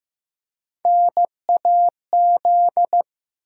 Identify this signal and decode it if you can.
Morse code, heard as a keyed tone, about 15 wpm, 700 Hz. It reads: NAZ
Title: USA Call Sign Postfixes - 3 Letters 15wpm